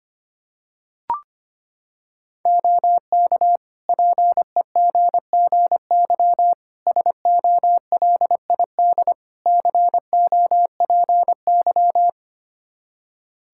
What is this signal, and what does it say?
Morse code, heard as a keyed tone, about 25 wpm, 700 Hz. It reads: OK PEGGY SOLID COPY
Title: Practice QSO - Exchange 3 - More Info - 25wpm